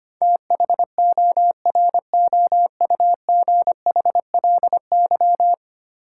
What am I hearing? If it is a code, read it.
Morse code, heard as a keyed tone, about 25 wpm, 700 Hz. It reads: THOROUGHLY
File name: 10 Letter Words - First 3 Letters Given - 25wpm